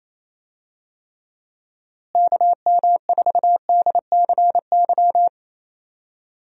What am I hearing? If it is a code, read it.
Morse code, heard as a keyed tone, about 28 wpm, 700 Hz. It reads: KM4DCY
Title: USA Call Signs - 6 Characters - Repeated 2x - 28wpm